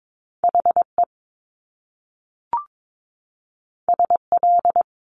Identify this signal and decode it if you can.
Morse code, heard as a keyed tone, about 22 wpm, 700 Hz. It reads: HE SL
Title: Top Two Letter Combinations in Words 22wpm